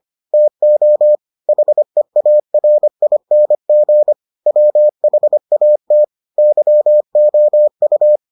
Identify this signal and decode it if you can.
Morse code, heard as a keyed tone, about 25 wpm, 600 Hz. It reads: TO HEARING WHAT YOU